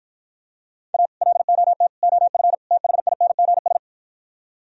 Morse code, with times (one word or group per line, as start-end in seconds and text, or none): 0.9-1.1 A
1.2-1.9 LOT
2.0-2.5 OF
2.7-3.8 THINGS